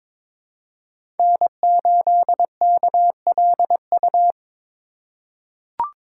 Morse code, heard as a keyed tone, about 23 wpm, 700 Hz. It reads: N8KLU
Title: USA Call Signs - 5 Characters - 4x Character Spacing - 22wpm